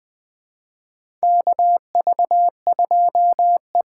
Morse code, heard as a keyed tone, about 20 wpm, 700 Hz. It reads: KV2E
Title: USA Call Signs - 4 Characters - 6x Character Spacing - 20wpm